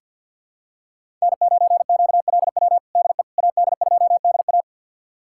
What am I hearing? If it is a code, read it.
Morse code, heard as a keyed tone, about 47 wpm, 700 Hz. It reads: N9QLW DE AD2DA
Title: USA Callsign DE Callsign - 5 Characters - 50wpm